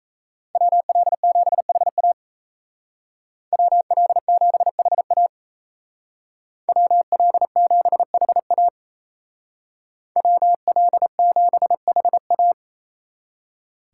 Morse code, heard as a keyed tone, about 37 wpm, 700 Hz. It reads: WL7HA WL7HA WL7HA WL7HA